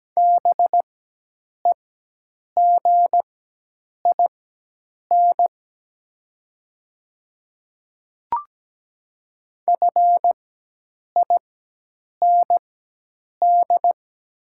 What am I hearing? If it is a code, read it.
Morse code, heard as a keyed tone, about 17 wpm, spaced as for 8 wpm, 700 Hz. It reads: BEGIN FIND
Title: 4x Character Spacing - Top 300 Words - 17wpm